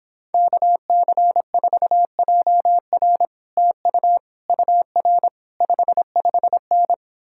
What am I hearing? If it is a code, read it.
Morse code, heard as a keyed tone, about 29 wpm, 700 Hz. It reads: KC4JR TU UR 55N